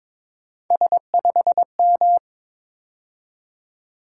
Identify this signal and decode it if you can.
Morse code, heard as a keyed tone, about 22 wpm, 700 Hz. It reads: S5M
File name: practice / 22wpm / Three Letter-Number Combinations - 22wpm